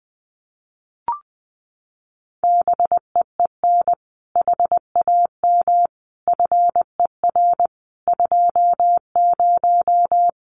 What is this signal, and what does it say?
Morse code, heard as a keyed tone, about 19 wpm, 700 Hz. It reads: BEEN HAM FER 20
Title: Common QSO Phrases 20wpm